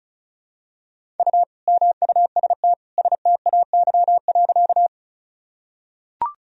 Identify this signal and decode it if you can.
Morse code, heard as a keyed tone, about 36 wpm, 700 Hz. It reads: U MUST STAY.
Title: Sentences from Top 500 Words and Abbreviations 35wpm